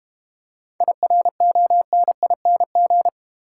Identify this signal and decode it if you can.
Morse code, heard as a keyed tone, about 32 wpm, 690 Hz. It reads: IRONING